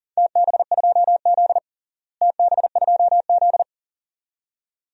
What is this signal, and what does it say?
Morse code, heard as a keyed tone, about 40 wpm, 700 Hz. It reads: TB2Z TB2Z
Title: DXCC-100 Call Signs - 4 Characters - Repeated 2x - 40wpm